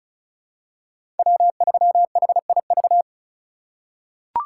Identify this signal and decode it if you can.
Morse code, heard as a keyed tone, about 34 wpm, 700 Hz. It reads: W3HIV
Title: USA Call Signs - 5 Characters - 4x Character Spacing - 35wpm